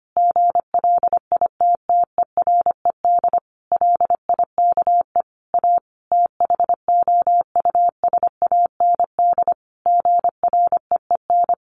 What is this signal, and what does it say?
Morse code, heard as a keyed tone, about 25 wpm, 700 Hz. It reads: GLITTERED LIKE A THOUSAND GREEN
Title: A Sound Of Thunder 25wpm